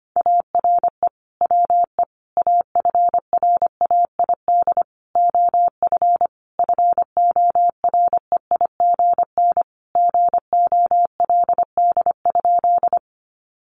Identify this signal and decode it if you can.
Morse code, heard as a keyed tone, about 25 wpm, 700 Hz. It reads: ARE WE AFRAID OF FOREIGN GOLD?
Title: Sentences from Top 1500 Words 25wpm